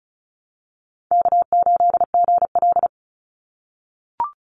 Morse code, heard as a keyed tone, about 34 wpm, 700 Hz. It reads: K8GL
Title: Call Signs - 35wpm